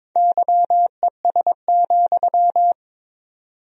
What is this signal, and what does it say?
Morse code, heard as a keyed tone, about 22 wpm, 700 Hz. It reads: YES,
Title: 4x Word-Spacing - Sentences from Top 500 Words - 22wpm